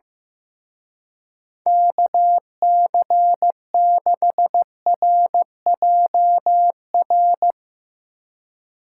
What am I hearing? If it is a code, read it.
Morse code, heard as a keyed tone, about 16 wpm, 700 Hz. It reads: KC6RJR
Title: Speed Racing - USA 2x3 Call Signs 15wpm